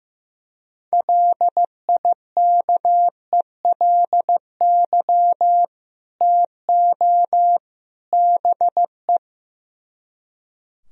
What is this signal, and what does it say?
Morse code, heard as a keyed tone, about 15 wpm, 700 Hz. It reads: LIKELY TO BE
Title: Speed-Racing - Sets of 3 Words 15wpm